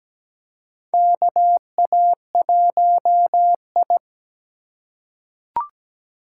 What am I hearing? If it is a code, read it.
Morse code, heard as a keyed tone, about 17 wpm, 700 Hz. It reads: KA1I
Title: USA Call Signs - 4 Characters - 4x Character Spacing - 17wpm